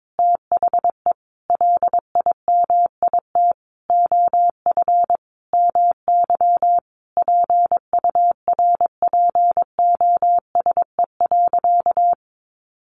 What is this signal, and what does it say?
Morse code, heard as a keyed tone, about 22 wpm, 700 Hz. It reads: THE LIMIT OF MY PURPOSE.
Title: Sentences from Top 1500 Words 22wpm